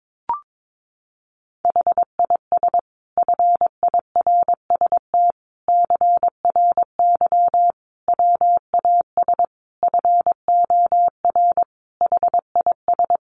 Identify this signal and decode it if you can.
Morse code, heard as a keyed tone, about 22 wpm, 700 Hz. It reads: HIS FIRST CRY WAS FOR HIS
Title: Sentences from Top 500 Words 22wpm